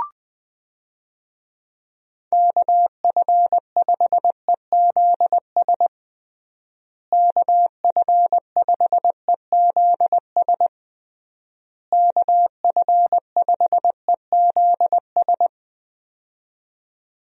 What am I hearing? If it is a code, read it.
Morse code, heard as a keyed tone, about 20 wpm, 700 Hz. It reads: KF5EZS KF5EZS KF5EZS
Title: USA Call Signs - 6 Characters - Repeated 3x - 20wpm